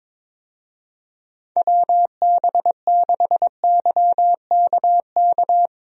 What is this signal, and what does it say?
Morse code, heard as a keyed tone, about 22 wpm, 700 Hz. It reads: WB6YKK
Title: USA Call Signs - 6 Characters - 22wpm